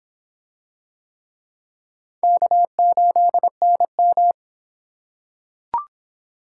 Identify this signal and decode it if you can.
Morse code, heard as a keyed tone, about 22 wpm, 700 Hz. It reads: K8NM